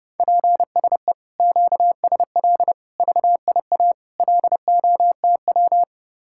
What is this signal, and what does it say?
Morse code, heard as a keyed tone, about 30 wpm, 700 Hz. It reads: PSE QSL VIA LOTW